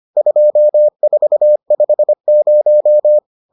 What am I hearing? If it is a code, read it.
Morse code, heard as a keyed tone, about 25 wpm, 600 Hz. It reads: 2450